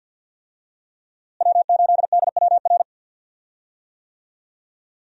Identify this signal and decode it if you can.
Morse code, heard as a keyed tone, about 50 wpm, 700 Hz. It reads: W8DWR
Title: USA Call Signs - 5 Characters - Repeated 3x - 50wpm